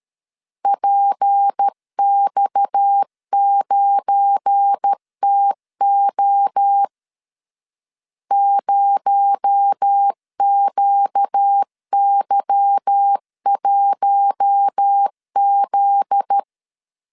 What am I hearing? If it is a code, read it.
Morse code, heard as a keyed tone, about 13 wpm, 800 Hz. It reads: PX9TO 0QY1Z
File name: LEZIONI / 8 YJ45 / 8 Y 12wpm 2228